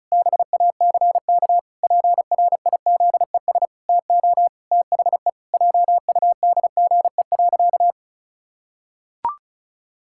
Morse code, heard as a keyed tone, about 36 wpm, 700 Hz. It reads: BACK PRIZES TO THE JUDGE.
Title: Pangram Challenge v2 - 35wpm